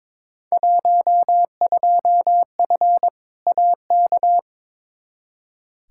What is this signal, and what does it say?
Morse code, heard as a keyed tone, about 22 wpm, 700 Hz. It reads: 12F AK
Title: Field Day Exchanges Encore - 22wpm